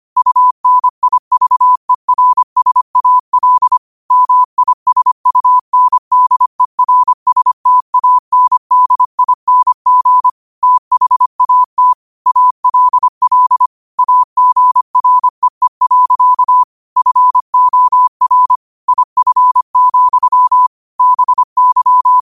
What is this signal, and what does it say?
Morse code, heard as a keyed tone, about 25 wpm, 1000 Hz. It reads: UNIVERSAL MISUNDERSTANDING THAT ALL AGREE. FOR IF, BY